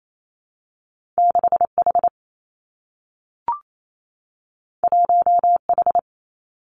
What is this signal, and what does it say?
Morse code, heard as a keyed tone, about 28 wpm, 700 Hz. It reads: 6 H 1 H
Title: Two Challenging Letter-Number Combinations 28wpm